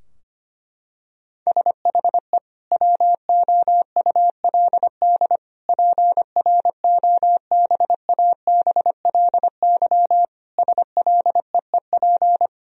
Morse code, heard as a keyed tone, about 25 wpm, 700 Hz. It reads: SHE WOULD PROBABLY SLEEP
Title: Sentences from Top 1000 Words - Review Part 4 - 25wpm